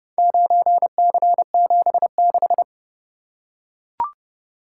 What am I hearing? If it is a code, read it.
Morse code, heard as a keyed tone, about 29 wpm, 700 Hz. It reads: 9C76